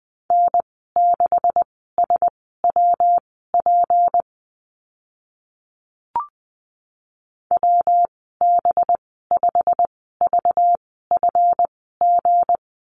Morse code, heard as a keyed tone, about 20 wpm, spaced as for 14 wpm, 700 Hz. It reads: N6SWP WB5VFG
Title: 193 Lesson 31g - Letter v - Call Signs 20wpm